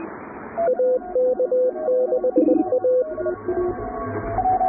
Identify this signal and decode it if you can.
Morse code, heard as a keyed tone, about 20 wpm, 510 Hz, with radio noise. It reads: AK6A
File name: chatham / 0216-15CW